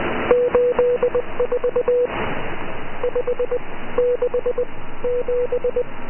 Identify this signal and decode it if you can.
Morse code, heard as a keyed tone, about 20 wpm, spaced as for 15 wpm, 490 Hz. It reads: 84 567